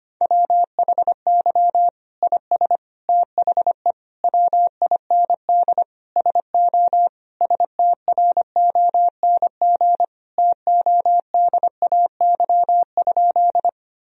Morse code, heard as a keyed tone, about 25 wpm, 700 Hz. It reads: WHY IS THE WIND SO STRONG TODAY?
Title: Sentences from Top 1000 Words - Review Part 2 - 25wpm